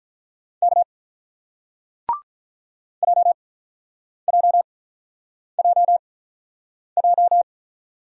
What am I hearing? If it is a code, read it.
Morse code, heard as a keyed tone, about 46 wpm, spaced as for 9 wpm, 700 Hz. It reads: K JJJJ